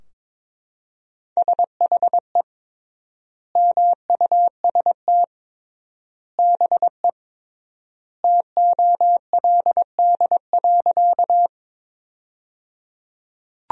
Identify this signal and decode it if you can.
Morse code, heard as a keyed tone, about 22 wpm, 700 Hz. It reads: SHE MUST BE TOLD.
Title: 3x Word-Spacing - Sentences from Top 400 Words - 22wpm